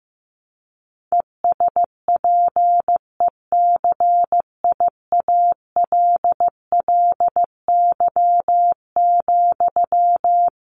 Morse code, heard as a keyed tone, about 15 wpm, 700 Hz. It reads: ESPECIALLY,
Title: Sentences from Top 700 Words 15wpm